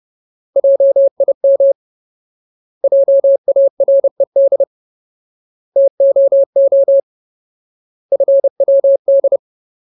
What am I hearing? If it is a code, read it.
Morse code, heard as a keyed tone, about 30 wpm, 550 Hz. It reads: JIM JARED TOO FWD
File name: Ultimate Word List / 30WPM Ultimate